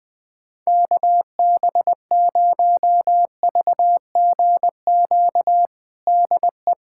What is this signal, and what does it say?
Morse code, heard as a keyed tone, about 20 wpm, 690 Hz. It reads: KB0VGQ DE
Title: USA Callsign DE Callsign 20wpm